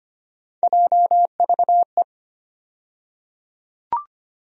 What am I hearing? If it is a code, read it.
Morse code, heard as a keyed tone, about 25 wpm, 700 Hz. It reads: JVE
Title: USA Call Sign Postfixes - 3 Letters 25wpm